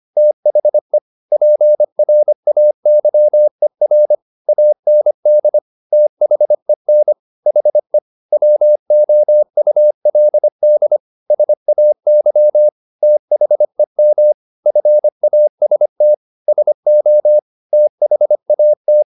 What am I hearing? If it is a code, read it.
Morse code, heard as a keyed tone, about 25 wpm, 600 Hz. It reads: THE PRAYER AND THEN HE WOULD SAY THEM FAST SO THAT
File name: the old man and the sea part 1 25wpm